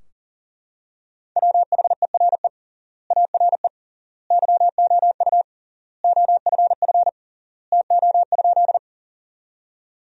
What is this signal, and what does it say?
Morse code, heard as a keyed tone, about 40 wpm, 700 Hz. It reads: WHERE ARE YOU OFF TO?
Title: 3x Word-Spacing - Sentences from Top 200 Words - 40wpm